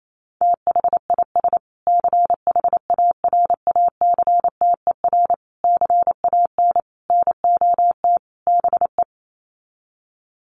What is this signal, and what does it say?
Morse code, heard as a keyed tone, about 28 wpm, 700 Hz. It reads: THIS CHARACTER CAN NOT BE
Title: Sets of 5 Words Encore2 28wpm